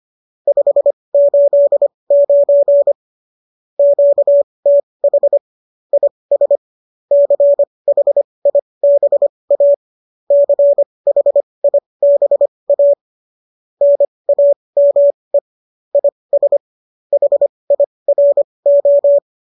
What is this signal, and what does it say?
Morse code, heard as a keyed tone, about 25 wpm, spaced as for 20 wpm, 570 Hz. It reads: 589 QTH IS CHIBA CHIBA NAME IS HIRO
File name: qso208 20